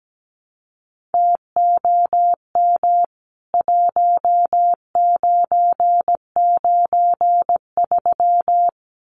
Morse code, heard as a keyed tone, about 17 wpm, 700 Hz. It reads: TOM 1993
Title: CWT Exchange 17wpm